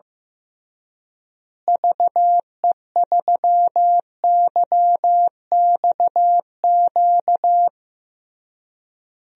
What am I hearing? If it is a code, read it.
Morse code, heard as a keyed tone, about 15 wpm, 700 Hz. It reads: VE3YXQ
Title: DXCC-100 Call Signs - 6 Characters - Repeated 3x - 15wpm